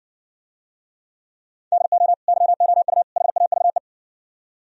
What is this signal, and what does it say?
Morse code, heard as a keyed tone, about 69 wpm, 700 Hz. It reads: DO YOU HAVE